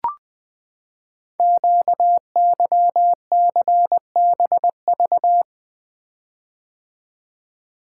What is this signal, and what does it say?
Morse code, heard as a keyed tone, about 20 wpm, 700 Hz. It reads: QYCBV
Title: Code Groups - 5 Letters - 20wpm